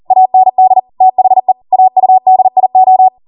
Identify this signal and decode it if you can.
Morse code, heard as a keyed tone, about 40 wpm, 750 Hz. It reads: AND THE AUDIO